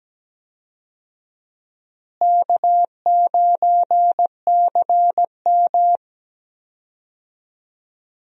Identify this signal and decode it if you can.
Morse code, heard as a keyed tone, about 17 wpm, 700 Hz. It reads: K9CM
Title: USA Call Signs - 4 Characters - Repeated 2x - 17wpm